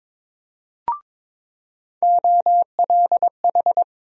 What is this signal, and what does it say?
Morse code, heard as a keyed tone, about 22 wpm, 700 Hz. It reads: OLH
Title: USA Call Sign Postfixes - 3 Letters 22wpm